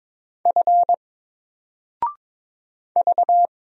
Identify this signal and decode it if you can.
Morse code, heard as a keyed tone, about 22 wpm, 700 Hz. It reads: F V
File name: Single-Letter-Number-22wpm